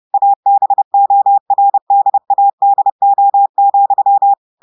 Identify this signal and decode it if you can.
Morse code, heard as a keyed tone, about 30 wpm, 800 Hz. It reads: ABORDADO,
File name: portugais texte en clair 2 vitesse 30 mots minutes